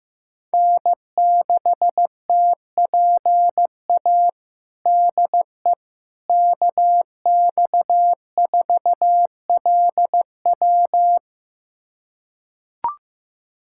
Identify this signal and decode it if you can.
Morse code, heard as a keyed tone, about 15 wpm, 700 Hz. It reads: N6TPA DE KX4LW
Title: USA Callsign DE Callsign - 5 Characters - 15wpm